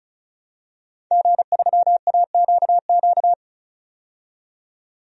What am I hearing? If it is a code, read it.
Morse code, heard as a keyed tone, about 36 wpm, 700 Hz. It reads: G3AQQ
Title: DXCC-100 Call Signs - 5 Characters - 35wpm